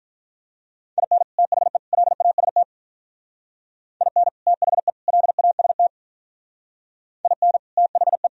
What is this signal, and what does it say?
Morse code, heard as a keyed tone, about 45 wpm, 700 Hz. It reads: IN THE LAST IN THE LAST IN THE